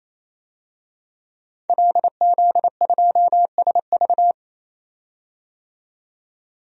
Morse code, heard as a keyed tone, about 28 wpm, 700 Hz. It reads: LZ2SV
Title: DXCC-100 Call Signs - 5 Characters - 28wpm